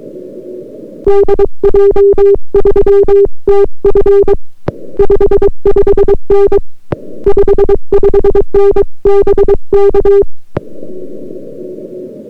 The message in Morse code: DJ3TF 55N 55N BK